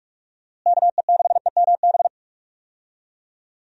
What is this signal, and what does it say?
Morse code, heard as a keyed tone, about 46 wpm, 700 Hz. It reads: KE6EMB
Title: USA Call Signs - 6 Characters - Repeated 2x - 45wpm